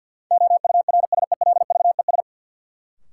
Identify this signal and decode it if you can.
Morse code, heard as a keyed tone, about 52 wpm, 700 Hz. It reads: OURSELVES
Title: Sentences from Top 1600 Words - New Word Drill Alt - 50wpm